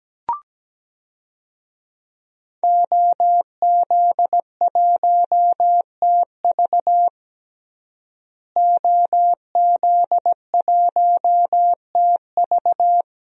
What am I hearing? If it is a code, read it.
Morse code, heard as a keyed tone, about 17 wpm, 700 Hz. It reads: OZ1TV OZ1TV
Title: DXCC-100 Call Signs - 5 Characters - Repeated 3x - 17wpm